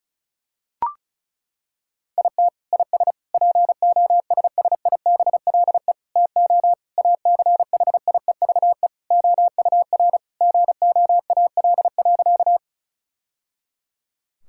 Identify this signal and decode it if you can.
Morse code, heard as a keyed tone, about 34 wpm, 700 Hz. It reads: IT IS POSSIBLE TO ACHIEVE OUR GOAL.
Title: Sentences from Top 1000 Words - Review Part 6 - 35wpm